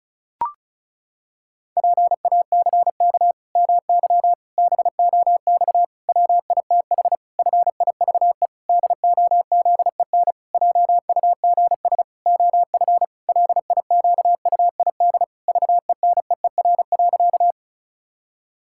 Morse code, heard as a keyed tone, about 36 wpm, 700 Hz. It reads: PACK MY BOX WITH FIVE DOZEN JUGS OF LIQUID VENEER.
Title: Pangram Challenge v2 - 35wpm